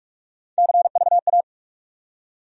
Morse code, heard as a keyed tone, about 43 wpm, 700 Hz. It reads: CVA
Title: USA Call Sign Postfixes - 3 Letters 45wpm